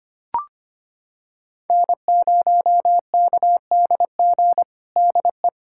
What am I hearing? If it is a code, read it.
Morse code, heard as a keyed tone, about 25 wpm, 700 Hz. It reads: N0KDG DE
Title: USA Callsign DE Callsign 25wpm